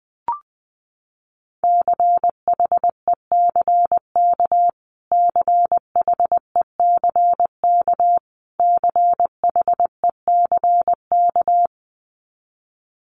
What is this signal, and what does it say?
Morse code, heard as a keyed tone, about 19 wpm, 700 Hz. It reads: CHECK CHECK CHECK
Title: Top 500 Words Repeated 20wpm